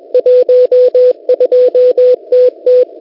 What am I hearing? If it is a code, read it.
Morse code, heard as a keyed tone, about 21 wpm, 500 Hz. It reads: 12TT
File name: Unid- -0701utc-4735khz